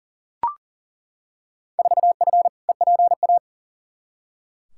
Morse code, heard as a keyed tone, about 38 wpm, 700 Hz. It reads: 4F EPA
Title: Field Day Exchanges - 40wpm